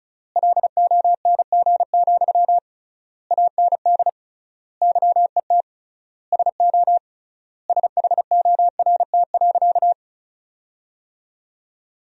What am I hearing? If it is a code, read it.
Morse code, heard as a keyed tone, about 36 wpm, 700 Hz. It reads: LONG, AND YET SO SHORT.
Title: 3x Word-Spacing - Sentences from Top 500 Words - 35wpm